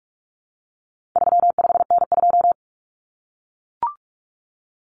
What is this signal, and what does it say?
Morse code, heard as a keyed tone, about 44 wpm, 700 Hz. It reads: 35N2
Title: Common Cut Numbers - 4-Digit 45wpm